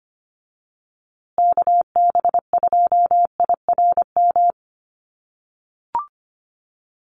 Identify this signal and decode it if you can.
Morse code, heard as a keyed tone, about 25 wpm, 700 Hz. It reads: KB2IRM
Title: USA Call Signs Repeated 2x 25wpm